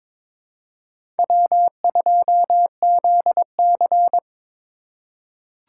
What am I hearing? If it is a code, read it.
Morse code, heard as a keyed tone, about 22 wpm, 700 Hz. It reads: W2ZC
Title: USA Call Signs - 4 Characters - 3x Character Spacing - 22wpm